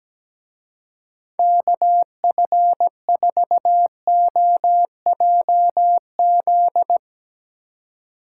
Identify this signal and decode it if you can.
Morse code, heard as a keyed tone, about 18 wpm, 700 Hz. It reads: KF4OJZ
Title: Speed Racing - USA Call Signs 17wpm